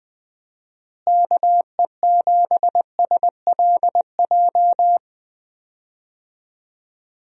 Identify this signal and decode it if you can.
Morse code, heard as a keyed tone, about 21 wpm, 700 Hz. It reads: KE7SLJ
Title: USA Call Signs - 6 Characters - Repeated 3x - 20wpm